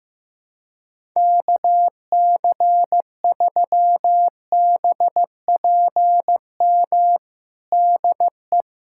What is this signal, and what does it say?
Morse code, heard as a keyed tone, about 15 wpm, 700 Hz. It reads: KC3BPM DE
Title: USA Callsign DE Callsign - 6 Characters - 15wpm